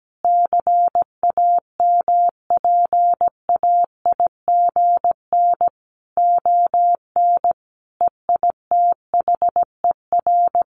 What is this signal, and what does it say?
Morse code, heard as a keyed tone, about 17 wpm, 700 Hz. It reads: CAMPAIGN ON EITHER
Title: Sentences from Top 1200 Words 17wpm